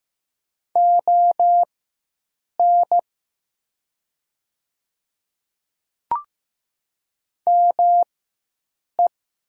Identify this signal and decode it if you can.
Morse code, heard as a keyed tone, about 15 wpm, spaced as for 7 wpm, 700 Hz. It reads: ON ME
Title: 4x Character Spacing - Top 200 Words - 15wpm